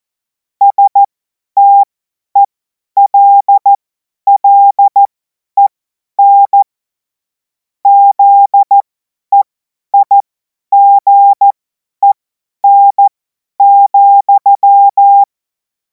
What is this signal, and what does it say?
Morse code, heard as a keyed tone, about 14 wpm, spaced as for 10 wpm, 800 Hz. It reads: STELLEN ZEIGEN,
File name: Morsekurs / Witze / Witz 24. Februar 2026 50 F70